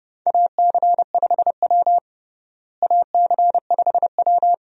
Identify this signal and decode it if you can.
Morse code, heard as a keyed tone, about 31 wpm, 700 Hz. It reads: AC5W AC5W
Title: USA Call Signs - 4 Characters - Repeated 2x - 30wpm